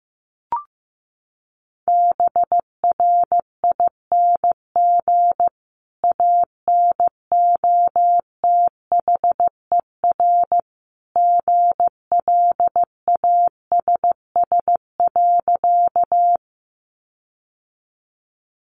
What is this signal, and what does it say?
Morse code, heard as a keyed tone, about 15 wpm, 700 Hz. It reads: BRING ANOTHER GLASS.
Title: Sentences from Top 1300 Words 15wpm